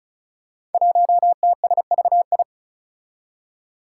Morse code, heard as a keyed tone, about 36 wpm, 700 Hz. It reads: 1TSVI
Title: Five Challenging Letter-Number Combinations 35wpm